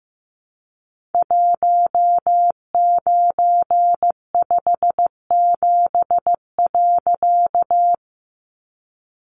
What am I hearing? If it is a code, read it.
Morse code, heard as a keyed tone, about 15 wpm, 700 Hz. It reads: 1957.
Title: A Sound Of Thunder 15wpm